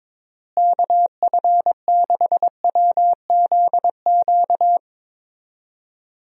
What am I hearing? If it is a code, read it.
Morse code, heard as a keyed tone, about 22 wpm, 700 Hz. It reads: KF6WZQ
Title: USA Call Signs - 6 Characters - Repeated 3x - 22wpm